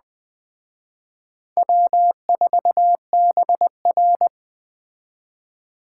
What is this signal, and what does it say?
Morse code, heard as a keyed tone, about 20 wpm, 700 Hz. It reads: W4BR
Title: Speed Racing - USA 1x2 Call Signs 17wpm